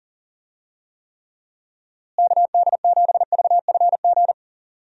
Morse code, heard as a keyed tone, about 40 wpm, 700 Hz. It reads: KD7VFG
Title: USA Call Signs - 6 Characters - Repeated 2x - 40wpm